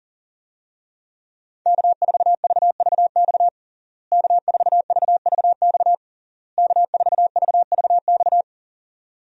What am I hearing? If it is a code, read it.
Morse code, heard as a keyed tone, about 40 wpm, 700 Hz. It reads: K4VVX K4VVX K4VVX
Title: USA Call Signs - 5 Characters - Repeated 3x - 40wpm